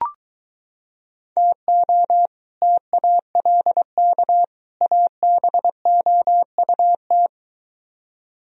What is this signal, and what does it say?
Morse code, heard as a keyed tone, about 24 wpm, 700 Hz. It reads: TO TALK ABOUT